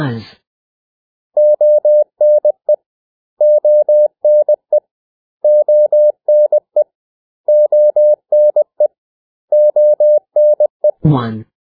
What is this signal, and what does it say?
Morse code, heard as a keyed tone, about 20 wpm, 600 Hz. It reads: ONE ONE ONE ONE ONE
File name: cw / 100 Common English Words